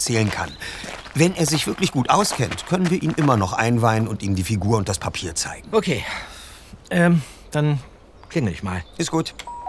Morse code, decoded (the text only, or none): none